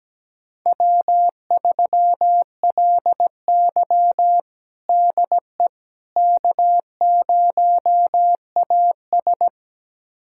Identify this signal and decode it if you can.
Morse code, heard as a keyed tone, about 17 wpm, 700 Hz. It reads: W3LY DE K0AS
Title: USA Callsign DE Callsign - 4 Characters - 17wpm